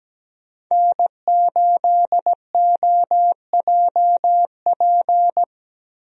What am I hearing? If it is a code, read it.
Morse code, heard as a keyed tone, about 17 wpm, 700 Hz. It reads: N8OJP
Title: USA Call Signs - 5 Characters - Repeated 3x - 17wpm